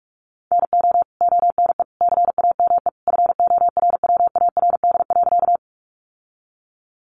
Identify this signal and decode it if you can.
Morse code, heard as a keyed tone, about 44 wpm, 700 Hz. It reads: NO ONE CAME FORWARD.
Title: Sentences from Top 1300 Words 45wpm